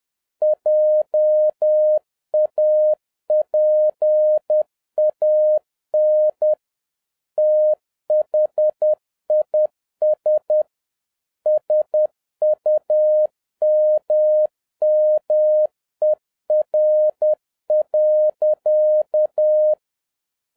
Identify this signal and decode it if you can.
Morse code, heard as a keyed tone, about 10 wpm, 610 Hz. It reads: JAPAN THIS SUMMER.